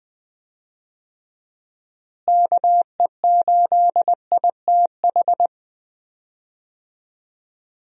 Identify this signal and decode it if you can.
Morse code, heard as a keyed tone, about 20 wpm, 700 Hz. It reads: KE8ITH
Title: USA Call Signs - 6 Characters - Repeated 2x - 20wpm